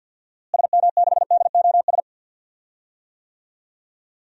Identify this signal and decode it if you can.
Morse code, heard as a keyed tone, about 51 wpm, 700 Hz. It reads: SM6DOS
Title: DXCC-100 Call Signs - 6 Characters - Repeated 2x - 50wpm